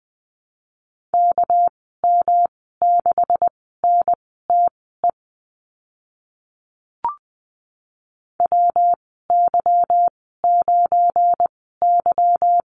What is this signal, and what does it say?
Morse code, heard as a keyed tone, about 20 wpm, spaced as for 15 wpm, 700 Hz. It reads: KM6NTE WY9Y
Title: 231 Lesson 37g - Letter x - Call Signs 20wpm